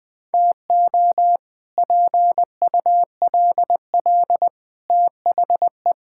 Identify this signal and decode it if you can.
Morse code, heard as a keyed tone, about 20 wpm, 700 Hz. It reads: TO PULL THE
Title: Sets of 8 Words - 20wpm